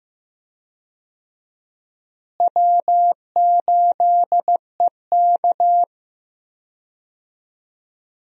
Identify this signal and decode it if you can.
Morse code, heard as a keyed tone, about 16 wpm, 700 Hz. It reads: W8EK
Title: USA Call Signs - 4 Characters - Repeated 2x - 15wpm